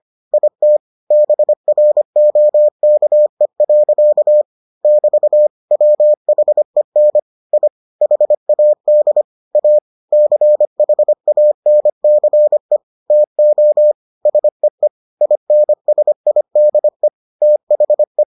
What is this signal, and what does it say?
Morse code, heard as a keyed tone, about 25 wpm, 600 Hz. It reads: IT BROKE. <BT> WHEN I HAD A CHANCE TO SEE INSIDE THE